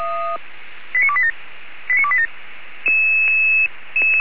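Five-tone selective call (ZVEI), heard as quick runs of five tones, tones of about 70 ms. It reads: 89287 89287
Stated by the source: ZVEI five-tone selective call received with an SDR.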